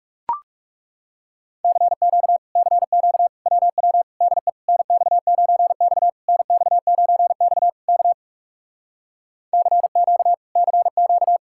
Morse code, heard as a keyed tone, about 40 wpm, 700 Hz. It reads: CQ CQ WW DE NX9X NX9X K CQ CQ